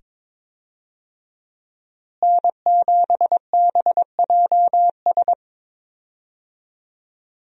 Morse code, heard as a keyed tone, about 22 wpm, 700 Hz. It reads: N7BJS
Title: USA Call Signs - 5 Characters - Repeated 2x - 22wpm